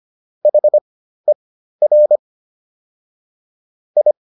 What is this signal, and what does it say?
Morse code, heard as a keyed tone, about 25 wpm, spaced as for 13 wpm, 600 Hz. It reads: HER I